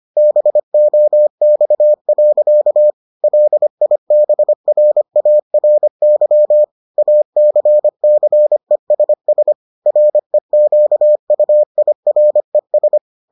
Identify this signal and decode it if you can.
Morse code, heard as a keyed tone, about 25 wpm, 600 Hz. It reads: BOX. LIBRARY ACCESS REQUIRES